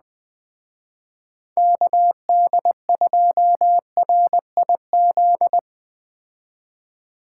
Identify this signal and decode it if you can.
Morse code, heard as a keyed tone, about 20 wpm, 700 Hz. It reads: KD2RIZ
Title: Speed Racing - USA 2x3 Call Signs 17wpm